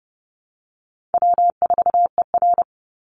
Morse code, heard as a keyed tone, about 30 wpm, 700 Hz. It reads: W4ER
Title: Easy Call Signs - 30wpm